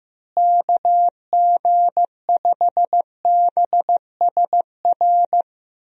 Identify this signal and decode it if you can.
Morse code, heard as a keyed tone, about 16 wpm, 700 Hz. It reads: KG5BSR
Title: USA Call Signs - 6 Characters - Repeated 3x - 15wpm